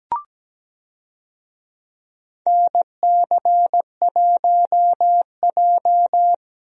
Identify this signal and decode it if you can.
Morse code, heard as a keyed tone, about 17 wpm, 700 Hz. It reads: NC1J
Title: USA Call Signs - 4 Characters - Repeated 3x - 17wpm